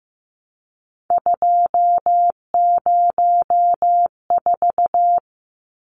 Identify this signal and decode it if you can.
Morse code, heard as a keyed tone, about 15 wpm, 700 Hz. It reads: 204
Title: Numbers - 3-Digit 15wpm